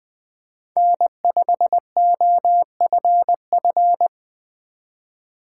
Morse code, heard as a keyed tone, about 20 wpm, 700 Hz. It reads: N5OFF
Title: USA Call Signs - 5 Characters - 4x Character Spacing - 20wpm